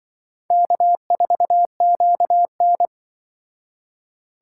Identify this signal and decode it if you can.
Morse code, heard as a keyed tone, about 24 wpm, 700 Hz. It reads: K4QN
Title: Speed Racing - USA Call Signs 20wpm